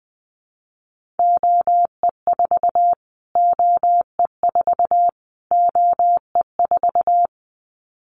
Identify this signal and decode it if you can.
Morse code, heard as a keyed tone, about 20 wpm, 700 Hz. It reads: OE4 OE4 OE4
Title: Callsign Prefixes for Top 25 DXCC - Repeated 3x - 20wpm